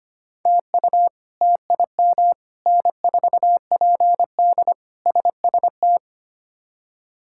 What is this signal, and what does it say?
Morse code, heard as a keyed tone, about 25 wpm, 700 Hz. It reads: TU TIM N4PD SST
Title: K1USN SST QSO Practice - 25wpm